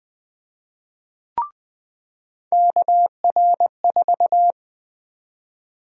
Morse code, heard as a keyed tone, about 20 wpm, 700 Hz. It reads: KR4
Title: USA Call Sign Prefixes - 20wpm